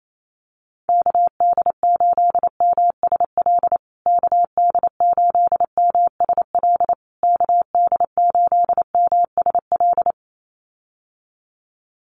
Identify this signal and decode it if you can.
Morse code, heard as a keyed tone, about 28 wpm, 700 Hz. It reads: KD8MSL KD8MSL KD8MSL
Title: USA Call Signs Repeated 3x 28wpm